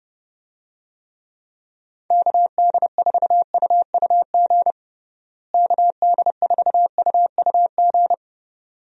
Morse code, heard as a keyed tone, about 29 wpm, 700 Hz. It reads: KD4UUG KD4UUG
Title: USA Call Signs - 6 Characters - Repeated 2x - 30wpm